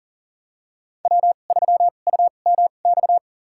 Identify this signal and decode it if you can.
Morse code, heard as a keyed tone, about 40 wpm, spaced as for 29 wpm, 700 Hz. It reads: W3UMX
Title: USA Call Signs - 5 Characters - 2x Character Spacing - 40wpm